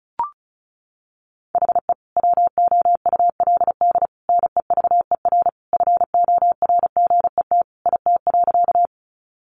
Morse code, heard as a keyed tone, about 34 wpm, 700 Hz. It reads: HE WOULD NEVER FORGET IT.
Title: Sentences from Top 1200 Words 35wpm